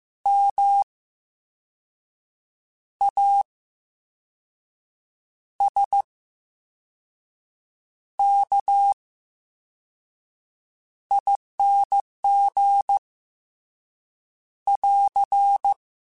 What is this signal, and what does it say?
Morse code, heard as a keyed tone, about 15 wpm, 800 Hz. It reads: M A S K I N G <AR>
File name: Special INGs-mp3s / Special INGs-mp3s / VerySlowToFaster Ing @13